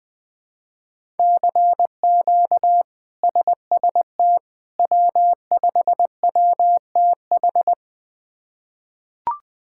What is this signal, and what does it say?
Morse code, heard as a keyed tone, about 20 wpm, 700 Hz. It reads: CQ SST W5WTH
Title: K1USN SST QSO Practice - 20wpm